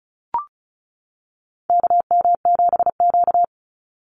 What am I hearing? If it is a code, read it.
Morse code, heard as a keyed tone, about 34 wpm, 700 Hz. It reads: KM7Q
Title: Call Signs - 35wpm